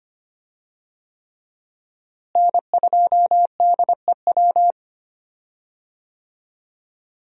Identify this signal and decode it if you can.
Morse code, heard as a keyed tone, about 25 wpm, 700 Hz. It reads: N2DEW
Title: USA Call Signs - 5 Characters - Repeated 3x - 25wpm